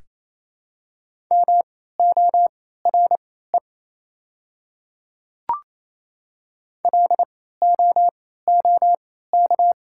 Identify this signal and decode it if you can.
Morse code, heard as a keyed tone, about 28 wpm, spaced as for 16 wpm, 700 Hz. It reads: MORE LOOK
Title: 3x Character Spacing - Top 100 Words - 28wpm